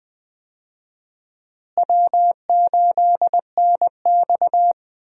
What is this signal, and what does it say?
Morse code, heard as a keyed tone, about 20 wpm, 700 Hz. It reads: W8NX
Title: USA Call Signs - 4 Characters - Repeated 3x - 20wpm